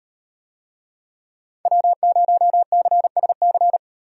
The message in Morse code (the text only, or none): W0CSC